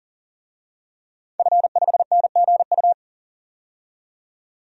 Morse code, heard as a keyed tone, about 37 wpm, 700 Hz. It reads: F5NGU
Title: DXCC-100 Call Signs - 5 Characters - 40wpm